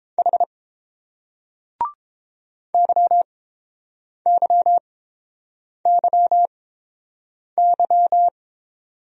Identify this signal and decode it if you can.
Morse code, heard as a keyed tone, about 30 wpm, spaced as for 7 wpm, 700 Hz. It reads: H YYYY